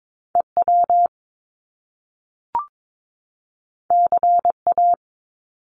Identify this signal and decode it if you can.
Morse code, heard as a keyed tone, about 22 wpm, 700 Hz. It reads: EW CA
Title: Top Two Letter Combinations in Words 22wpm